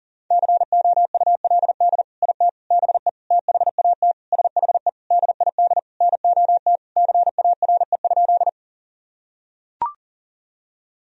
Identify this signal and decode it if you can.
Morse code, heard as a keyed tone, about 40 wpm, 700 Hz. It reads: COULD IT BE THAT SHE DID NOT CARE?
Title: Sentences from Top 1000 Words - Review Part 1 - 40wpm